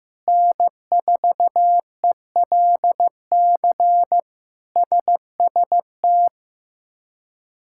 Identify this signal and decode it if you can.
Morse code, heard as a keyed tone, about 16 wpm, 700 Hz. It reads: N4ELC SST
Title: K1USN SST QSO Practice - 15wpm